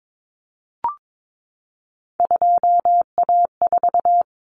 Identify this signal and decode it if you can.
Morse code, heard as a keyed tone, about 22 wpm, 700 Hz. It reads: 2A4
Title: Common Cut Numbers - 3-Digit 22wpm